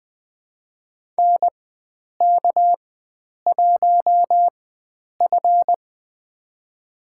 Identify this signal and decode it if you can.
Morse code, heard as a keyed tone, about 20 wpm, spaced as for 9 wpm, 700 Hz. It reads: NK1F
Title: USA Call Signs - 4 Characters - 4x Character Spacing - 20wpm